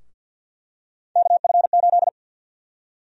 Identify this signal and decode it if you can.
Morse code, heard as a keyed tone, about 52 wpm, 700 Hz. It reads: KF8
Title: USA Call Sign Prefixes - 50wpm